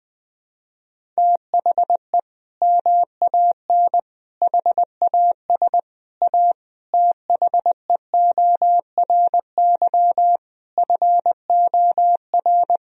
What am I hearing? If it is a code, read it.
Morse code, heard as a keyed tone, about 20 wpm, 700 Hz. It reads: THE MAN HAS A THEORY FOR
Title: Sentences from Top 1000 Words - Review Part 3 - 20wpm